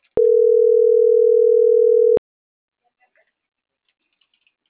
Ringback tone (North America), 440 + 480 Hz.